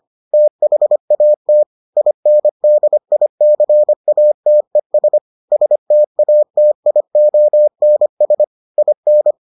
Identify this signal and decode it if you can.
Morse code, heard as a keyed tone, about 25 wpm, 600 Hz. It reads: THAT INDICATES STATIONS IN